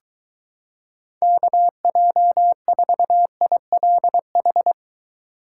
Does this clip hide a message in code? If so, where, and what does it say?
Yes: Morse code from 1.2 to 4.7 s, heard as a keyed tone, about 23 wpm, 700 Hz. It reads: KJ4ILH